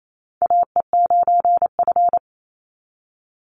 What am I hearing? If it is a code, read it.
Morse code, heard as a keyed tone, about 28 wpm, 700 Hz. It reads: AE9F
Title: NA SOTA Chaser Callsigns 28wpm